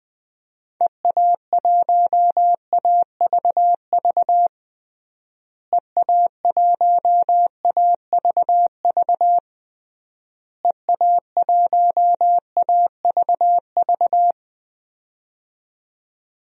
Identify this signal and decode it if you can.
Morse code, heard as a keyed tone, about 20 wpm, 700 Hz. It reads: EA1AVV EA1AVV EA1AVV
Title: DXCC-100 Call Signs - 6 Characters - Repeated 3x - 20wpm